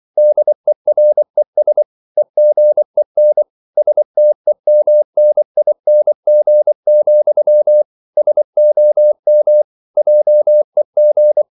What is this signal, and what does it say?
Morse code, heard as a keyed tone, about 24 wpm, 600 Hz. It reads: DERES EGEN STEMNING, SOM JEG
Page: Ferie 24 wpm